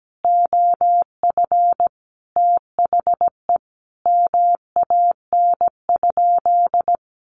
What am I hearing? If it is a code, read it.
Morse code, heard as a keyed tone, about 17 wpm, 700 Hz. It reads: OF THE MAN?
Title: Sentences from Top 1500 Words 17wpm